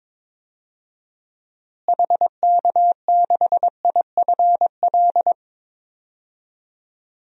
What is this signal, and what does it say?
Morse code, heard as a keyed tone, about 22 wpm, 700 Hz. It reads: HK6IFL
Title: DXCC-100 Call Signs - 6 Characters - Repeated 2x - 22wpm